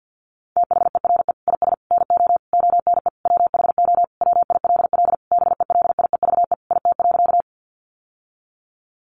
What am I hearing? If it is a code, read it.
Morse code, heard as a keyed tone, about 50 wpm, 700 Hz. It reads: THERE IS NO ONE WHO WILL BELIEVE IT.
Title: Sentences from Top 1000 Words 50wpm